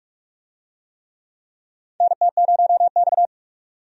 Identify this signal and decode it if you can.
Morse code, heard as a keyed tone, about 46 wpm, 700 Hz. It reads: NT0X
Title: USA Call Signs - 4 Characters - 45wpm